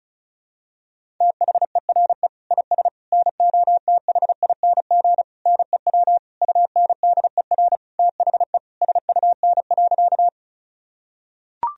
Morse code, heard as a keyed tone, about 36 wpm, 700 Hz. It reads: THERE IS NOTHING NEW UNDER THE SUN.
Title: Sentences from Top 1000 Words - Review Part 1 - 35wpm